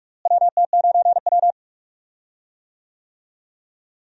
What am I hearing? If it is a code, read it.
Morse code, heard as a keyed tone, about 48 wpm, 700 Hz. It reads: WT9W
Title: USA Call Signs - 4 Characters - Repeated 3x - 45wpm